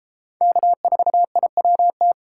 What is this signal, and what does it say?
Morse code, heard as a keyed tone, about 30 wpm, 700 Hz. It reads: K4IWT